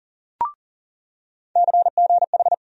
Morse code, heard as a keyed tone, about 40 wpm, 700 Hz. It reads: CGH
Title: USA Call Sign Postfixes - 3 Letters 40wpm